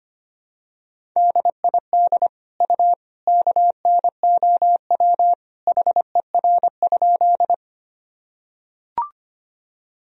Morse code, heard as a keyed tone, about 25 wpm, 700 Hz. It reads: DID U KNOW HER?
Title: Sentences from Top 100 Words and Abbreviations 25wpm